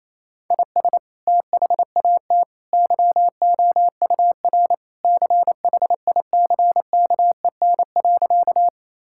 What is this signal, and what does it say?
Morse code, heard as a keyed tone, about 27 wpm, 700 Hz. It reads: IS THAT YOUR CHICKEN.
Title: Sentences from Top 1000 Words - Review Part 5 - 28wpm